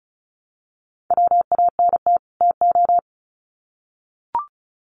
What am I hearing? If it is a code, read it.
Morse code, heard as a keyed tone, about 34 wpm, 700 Hz. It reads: WANT TO
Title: Sets of 2 Words 35wpm